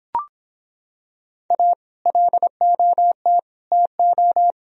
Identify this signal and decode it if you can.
Morse code, heard as a keyed tone, about 26 wpm, 700 Hz. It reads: A LOT TO